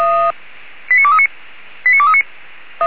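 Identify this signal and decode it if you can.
Five-tone selective call (ZVEI), heard as quick runs of five tones, tones of about 70 ms. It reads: 89239 89239
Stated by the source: Five-tone ZVEI selective call received by SDR.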